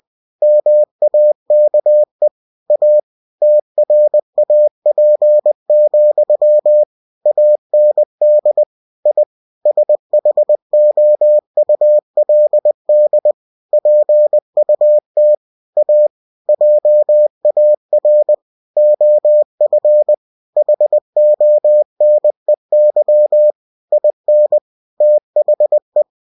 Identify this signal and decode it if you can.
Morse code, heard as a keyed tone, about 20 wpm, 600 Hz. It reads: MAKE A TRAP, AND I SHOULD PUT A JAR OF HONEY IN THE
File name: cw-books / winnie-the-pooh / winnie-20wpm-page-0036